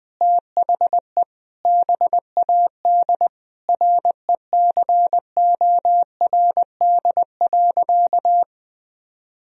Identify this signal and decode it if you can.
Morse code, heard as a keyed tone, about 20 wpm, 700 Hz. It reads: THE BAD RECORD.
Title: Sentences from Top 1000 Words - Review Part 3 - 20wpm